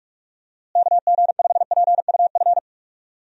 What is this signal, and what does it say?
Morse code, heard as a keyed tone, about 46 wpm, 700 Hz. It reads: KG5PUF
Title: USA Call Signs - 6 Characters - Repeated 2x - 45wpm